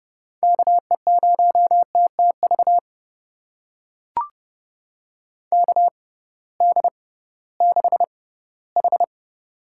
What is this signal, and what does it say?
Morse code, heard as a keyed tone, about 30 wpm, spaced as for 18 wpm, 700 Hz. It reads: KE0TTV KD6H
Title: USA Call Signs - 6 Characters - 6x Character Spacing - 30wpm